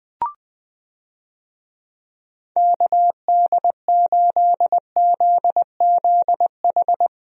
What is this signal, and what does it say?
Morse code, heard as a keyed tone, about 20 wpm, 700 Hz. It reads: KD8ZZH
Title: USA Call Signs - 6 Characters - 20wpm